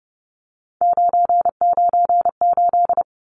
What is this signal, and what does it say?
Morse code, heard as a keyed tone, about 30 wpm, 700 Hz. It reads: 998
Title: Numbers - 3-Digit 30wpm